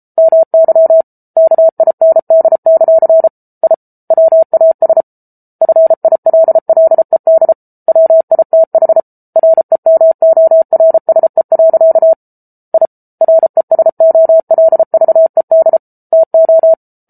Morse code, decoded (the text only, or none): MY KIND; I WAS FILLED WITH REMORSE. I RESOLVED TO